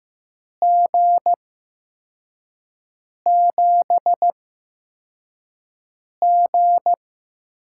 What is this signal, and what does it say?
Morse code, heard as a keyed tone, about 15 wpm, spaced as for 4 wpm, 700 Hz. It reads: G7G
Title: USA Call Signs - 4 Characters - 8x Character Spacing - 15wpm